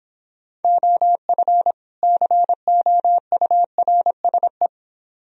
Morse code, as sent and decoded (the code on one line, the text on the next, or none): --- ..-. / -.-. --- ..- .-. ... .
OF COURSE